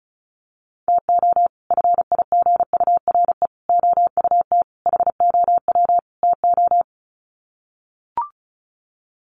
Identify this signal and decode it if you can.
Morse code, heard as a keyed tone, about 34 wpm, 700 Hz. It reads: TO FIGURE OUT HOW TO
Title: Sets of 5 Words 35wpm